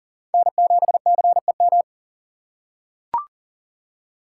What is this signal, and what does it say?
Morse code, heard as a keyed tone, about 40 wpm, 700 Hz. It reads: N7CEM
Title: USA Call Signs - 5 Characters - 4x Character Spacing - 40wpm